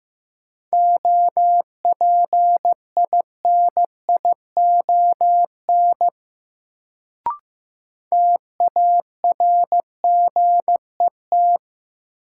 Morse code, heard as a keyed tone, about 15 wpm, 700 Hz. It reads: OPINION TARGET
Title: Sentences from Top 1200 Words - New Word Drill Alt - 15wpm